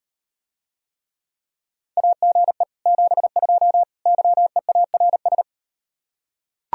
Morse code, heard as a keyed tone, about 39 wpm, 700 Hz. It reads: AGE 72 YEARS